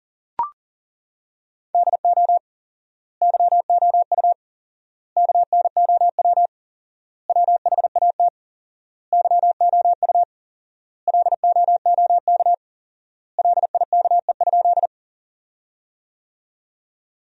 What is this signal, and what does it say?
Morse code, heard as a keyed tone, about 40 wpm, 700 Hz. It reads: DO YOU KNOW WHAT YOU LOOK LIKE?
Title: 4x Word-Spacing - Sentences from Top 100 Words - 40wpm